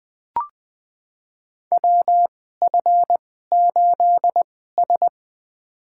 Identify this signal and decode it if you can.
Morse code, heard as a keyed tone, about 20 wpm, spaced as for 14 wpm, 700 Hz. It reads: WF8S
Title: USA Call Signs - 4 Characters - 2x Character Spacing - 20wpm